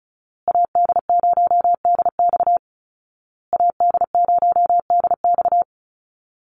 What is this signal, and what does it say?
Morse code, heard as a keyed tone, about 34 wpm, 700 Hz. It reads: AD0DX AD0DX
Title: Call Signs Repeated 2x - 35wpm